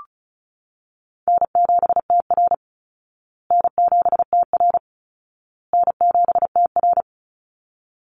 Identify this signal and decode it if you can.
Morse code, heard as a keyed tone, about 34 wpm, 700 Hz. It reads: N7TR N7TR N7TR
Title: Call Signs Repeated 2x - 35wpm